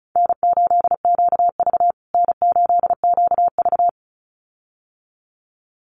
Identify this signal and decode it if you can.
Morse code, heard as a keyed tone, about 34 wpm, 700 Hz. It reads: N8QV N8QV
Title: USA Call Signs Repeated 2x 35wpm